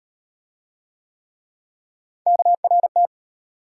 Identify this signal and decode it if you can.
Morse code, heard as a keyed tone, about 34 wpm, 700 Hz. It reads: KRT